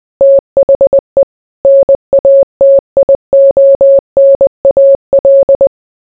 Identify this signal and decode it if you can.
Morse code, heard as a keyed tone, about 20 wpm, 550 Hz. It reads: THE NATIONAL